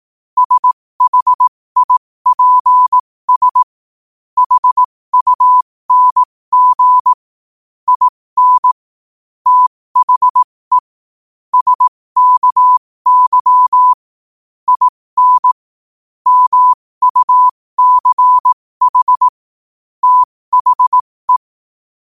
Morse code, heard as a keyed tone, about 18 wpm, spaced as for 15 wpm, 1000 Hz. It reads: SHIPS HUNG IN THE SKY IN MUCH THE
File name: qotdfiles / 15wpm